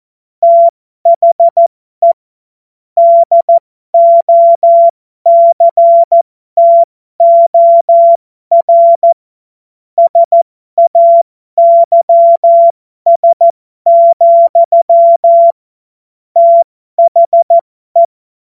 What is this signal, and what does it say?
Morse code, heard as a keyed tone, about 14 wpm, spaced as for 12 wpm, 680 Hz. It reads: THE DOCTOR SAYS, THE